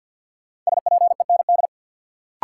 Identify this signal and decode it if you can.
Morse code, heard as a keyed tone, about 53 wpm, 700 Hz. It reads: SPEND
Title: Sets of 5 Words - New Word Drill - 50wpm